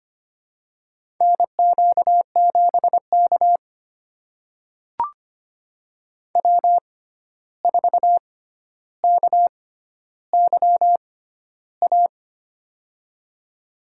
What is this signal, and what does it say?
Morse code, heard as a keyed tone, about 25 wpm, spaced as for 12 wpm, 700 Hz. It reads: NQ7K W4KYA
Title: USA Call Signs - 6x Character Spacing - 25wpm